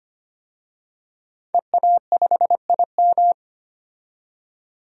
Morse code, heard as a keyed tone, about 25 wpm, 700 Hz. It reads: EA5IM
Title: DXCC-100 Call Signs - 5 Characters - 25wpm